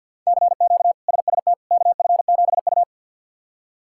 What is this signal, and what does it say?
Morse code, heard as a keyed tone, about 52 wpm, 690 Hz. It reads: CQ SST KF7U
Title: K1USN SST QSO Practice - 50wpm